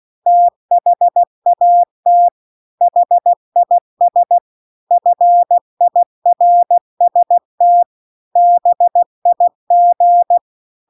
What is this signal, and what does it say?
Morse code, heard as a keyed tone, about 16 wpm, 700 Hz. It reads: THAT HIS FIRST BIG